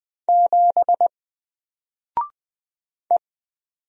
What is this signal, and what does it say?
Morse code, heard as a keyed tone, about 20 wpm, 700 Hz. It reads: 7 E
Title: Single Character (Letter:Number:Punctuation:Prosign) 20wpm